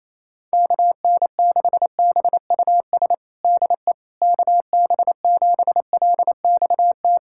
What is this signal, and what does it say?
Morse code, heard as a keyed tone, about 28 wpm, 700 Hz. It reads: KN6BUS DE KB7LXT
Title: USA Callsign DE Callsign - 6 Characters - 28wpm